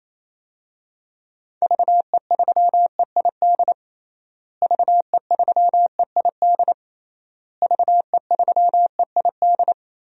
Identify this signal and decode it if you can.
Morse code, heard as a keyed tone, about 27 wpm, 700 Hz. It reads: VE3EID VE3EID VE3EID
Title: DXCC-100 Call Signs - 6 Characters - Repeated 3x - 28wpm